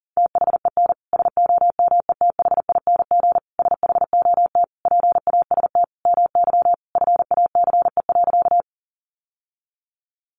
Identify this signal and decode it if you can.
Morse code, heard as a keyed tone, about 40 wpm, 700 Hz. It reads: THEN SOMETHING SHOT PAST MY FACE.
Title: Sentences from Top 1100 Words 40wpm